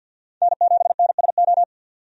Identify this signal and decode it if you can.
Morse code, heard as a keyed tone, about 51 wpm, 700 Hz. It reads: N7NSO
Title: USA Call Signs - 5 Characters - 50wpm